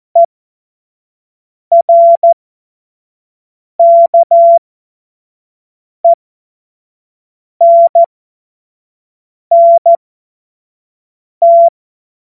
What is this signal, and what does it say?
Morse code, heard as a keyed tone, about 14 wpm, spaced as for 5 wpm, 680 Hz. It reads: ERKENNT